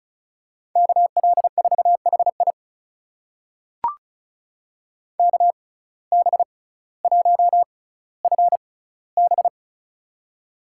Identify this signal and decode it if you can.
Morse code, heard as a keyed tone, about 36 wpm, spaced as for 18 wpm, 700 Hz. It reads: KL4HI KB1FB
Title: USA Call Signs - 5 Characters - 6x Character Spacing - 35wpm